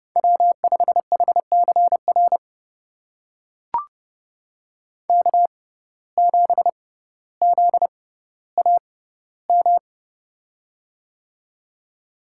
Morse code, heard as a keyed tone, about 30 wpm, spaced as for 15 wpm, 700 Hz. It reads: W5HCR K7ZAM
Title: USA Call Signs - 5 Characters - 6x Character Spacing - 30wpm